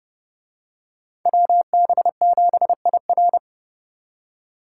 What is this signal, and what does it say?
Morse code, heard as a keyed tone, about 30 wpm, 700 Hz. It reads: WB7IR